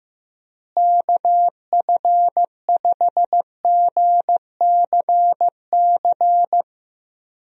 Morse code, heard as a keyed tone, about 15 wpm, 700 Hz. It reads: KF5GCC
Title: USA Call Signs - 6 Characters - Repeated 2x - 15wpm